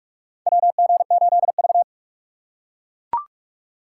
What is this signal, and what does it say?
Morse code, heard as a keyed tone, about 42 wpm, 690 Hz. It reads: WG8V
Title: USA Call Signs - 4x Character Spacing - 45wpm